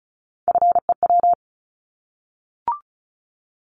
Morse code, heard as a keyed tone, about 36 wpm, 700 Hz. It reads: FEW
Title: Multi-letter Phonemes 35wpm